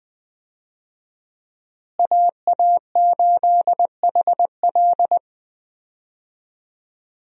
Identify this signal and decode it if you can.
Morse code, heard as a keyed tone, about 20 wpm, 700 Hz. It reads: AA8HL
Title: USA Call Signs - 5 Characters - Repeated 3x - 20wpm